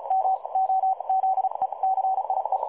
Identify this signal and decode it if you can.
Morse code, heard as a keyed tone, about 35 wpm, spaced as for 30 wpm, 760 Hz, with radio noise. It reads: MOZECIE
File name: dlazaawansowanych / nagraniazpasma / websdr recording 3565.0kHz